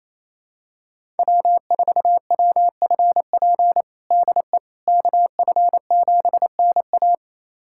Morse code, heard as a keyed tone, about 28 wpm, 700 Hz. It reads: W4WFP DE KF7NA
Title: USA Callsign DE Callsign - 5 Characters - 28wpm